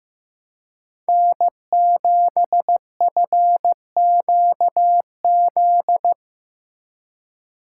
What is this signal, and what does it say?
Morse code, heard as a keyed tone, about 15 wpm, 700 Hz. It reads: N7FQZ
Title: USA Call Signs - 5 Characters - 3x Character Spacing - 15wpm